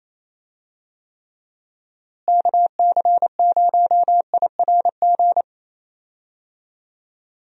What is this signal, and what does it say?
Morse code, heard as a keyed tone, about 28 wpm, 700 Hz. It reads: KC0IRG
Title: USA Call Signs - 6 Characters - Repeated 2x - 28wpm